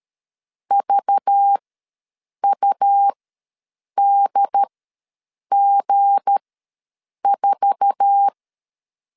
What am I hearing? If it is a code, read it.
Morse code, heard as a keyed tone, about 13 wpm, spaced as for 7 wpm, 800 Hz. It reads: VUDG4
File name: LEZIONI / 10 SIGNS / 10D COMA ws 12wpm 3618